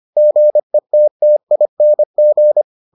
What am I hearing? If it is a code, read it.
Morse code, heard as a keyed tone, about 25 wpm, 600 Hz. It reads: GETTING